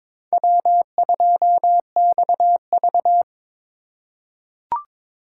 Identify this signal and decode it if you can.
Morse code, heard as a keyed tone, about 21 wpm, 700 Hz. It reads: W2XV
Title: USA Call Signs - 3x Character Spacing - 22wpm